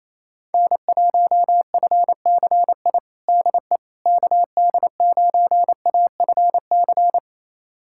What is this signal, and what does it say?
Morse code, heard as a keyed tone, about 27 wpm, 700 Hz. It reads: N1FCI DE KD9AFC
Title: USA Callsign DE Callsign 28wpm